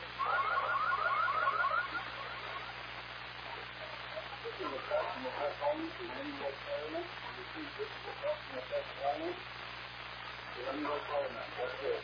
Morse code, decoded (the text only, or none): none